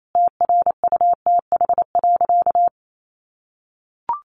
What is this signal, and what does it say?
Morse code, heard as a keyed tone, about 28 wpm, 700 Hz. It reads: TRUTH.
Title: Sentences from Top 1300 Words 28wpm